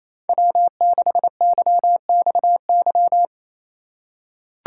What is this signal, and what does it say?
Morse code, heard as a keyed tone, about 28 wpm, 700 Hz. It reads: W6YXY
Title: USA Call Signs - 5 Characters - 4x Character Spacing - 28wpm